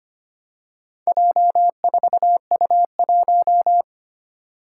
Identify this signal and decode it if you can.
Morse code, heard as a keyed tone, about 25 wpm, 700 Hz. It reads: J4U1
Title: Four Challenging Letter-Number Combinations 25wpm